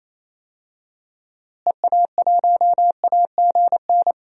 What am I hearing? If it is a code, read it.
Morse code, heard as a keyed tone, about 27 wpm, 700 Hz. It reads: EA1AGN
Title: DXCC-100 Call Signs - 6 Characters - Repeated 3x - 28wpm